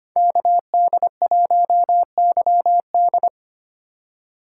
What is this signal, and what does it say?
Morse code, heard as a keyed tone, about 25 wpm, 700 Hz. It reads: KD1YD
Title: USA Call Signs - 5 Characters - 2x Character Spacing - 25wpm